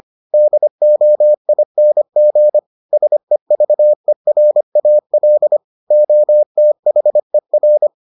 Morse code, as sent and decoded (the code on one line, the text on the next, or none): -.. --- .. -. --. / ... ...- . .-. .- .-.. / --- - .... .-.
DOING SEVERAL OTHER